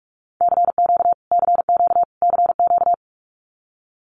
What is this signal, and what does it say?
Morse code, heard as a keyed tone, about 44 wpm, 700 Hz. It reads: CQ CQ CQ
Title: Abbreviations Repeated 3x - 45wpm